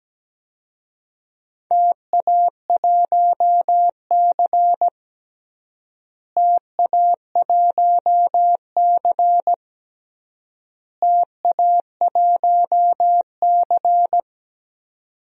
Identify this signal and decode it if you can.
Morse code, heard as a keyed tone, about 17 wpm, 700 Hz. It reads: TA1C TA1C TA1C
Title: DXCC-100 Call Signs - 4 Characters - Repeated 3x - 17wpm